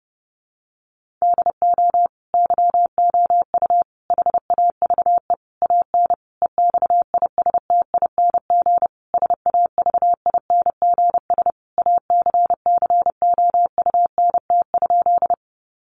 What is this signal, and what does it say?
Morse code, heard as a keyed tone, about 30 wpm, 700 Hz. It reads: DO YOU HAVE AN EXISTING SAVINGS ACCOUNT?
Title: Sentences from Top 1500 Words 30wpm